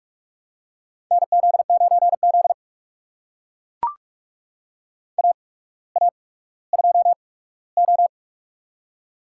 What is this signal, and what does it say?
Morse code, heard as a keyed tone, about 44 wpm, spaced as for 19 wpm, 700 Hz. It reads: NZ9Z AA2O
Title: USA Call Signs - 8x Character Spacing - 45wpm